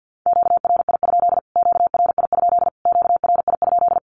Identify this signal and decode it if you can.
Morse code, heard as a keyed tone, about 48 wpm, 700 Hz. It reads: QRI? QRI? QRI?
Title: Q-Codes Repeated 3x 50wpm